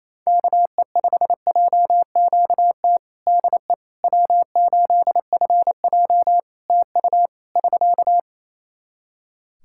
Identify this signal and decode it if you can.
Morse code, heard as a keyed tone, about 28 wpm, 700 Hz. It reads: KE5JQT DE W8FJ TU <SK>
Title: Practice QSO - Exchange 4 - Ending - 28wpm